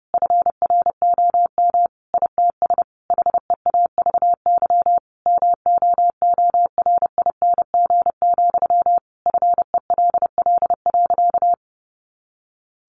Morse code, heard as a keyed tone, about 30 wpm, 700 Hz. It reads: FROM ITS HEAVY MOORING, FELL.
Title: A Sound Of Thunder 30wpm